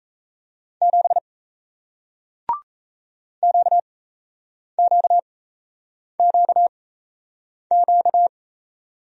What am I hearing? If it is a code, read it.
Morse code, heard as a keyed tone, about 38 wpm, spaced as for 8 wpm, 700 Hz. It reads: 7 QQQQ